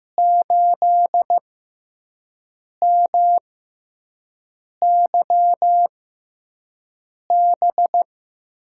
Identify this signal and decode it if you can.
Morse code, heard as a keyed tone, about 15 wpm, spaced as for 5 wpm, 700 Hz. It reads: 8MYB
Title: USA Call Signs - 6 Characters - 6x Character Spacing - 15wpm